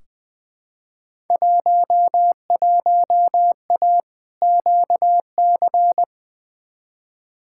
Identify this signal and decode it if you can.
Morse code, heard as a keyed tone, about 21 wpm, 700 Hz. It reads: 11A QC
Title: Field Day Exchanges Encore - 20wpm